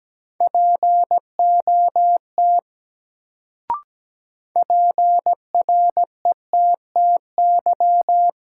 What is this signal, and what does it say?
Morse code, heard as a keyed tone, about 17 wpm, 700 Hz. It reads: POT PRETTY
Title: Sentences from Top 1000 Words - New Word Drill - 17wpm